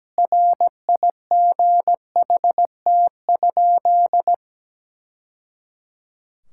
Morse code, heard as a keyed tone, about 17 wpm, 700 Hz. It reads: RIGHT?
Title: Sentences from Top 1000 Words - Review Part 5 - 17wpm